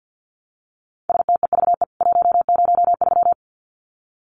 Morse code, heard as a keyed tone, about 50 wpm, 700 Hz. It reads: STEVE 103
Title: CWT Exchange 50wpm